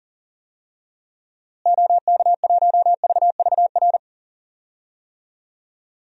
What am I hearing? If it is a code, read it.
Morse code, heard as a keyed tone, about 40 wpm, 700 Hz. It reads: OK1VVR
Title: DXCC-100 Call Signs - 6 Characters - Repeated 3x - 40wpm